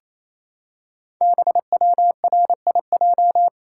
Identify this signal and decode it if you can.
Morse code, heard as a keyed tone, about 27 wpm, 700 Hz. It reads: BWRIJ
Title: Code Groups - 5 Letters - 28wpm